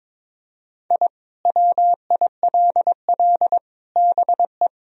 Morse code, heard as a keyed tone, about 21 wpm, 700 Hz. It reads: I WILL BE
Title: Sentences from Top 1000 Words - Review Part 3 - 22wpm